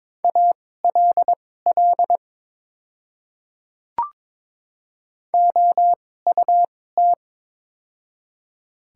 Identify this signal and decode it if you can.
Morse code, heard as a keyed tone, about 22 wpm, spaced as for 16 wpm, 700 Hz. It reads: ALL OUT
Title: 2x Character Spacing - Top 300 Words - 22wpm